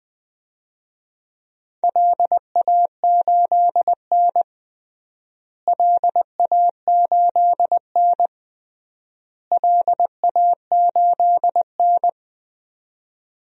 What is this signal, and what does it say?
Morse code, heard as a keyed tone, about 21 wpm, 700 Hz. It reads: LA8N LA8N LA8N
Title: DXCC-100 Call Signs - 4 Characters - Repeated 3x - 20wpm